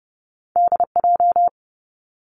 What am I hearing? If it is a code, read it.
Morse code, heard as a keyed tone, about 30 wpm, 700 Hz. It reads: D J
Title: Two Challenging Letter-Number Combinations 30wpm